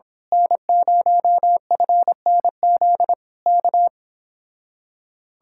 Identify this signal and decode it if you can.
Morse code, heard as a keyed tone, about 25 wpm, 700 Hz. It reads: N0FNZ K
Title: Speed-Racing Common QSO Phrases Encore - 22wpm